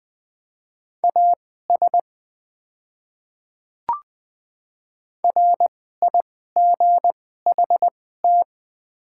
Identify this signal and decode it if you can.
Morse code, heard as a keyed tone, about 20 wpm, spaced as for 14 wpm, 700 Hz. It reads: AS RIGHT
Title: 2x Character Spacing - Top 200 Words - 20wpm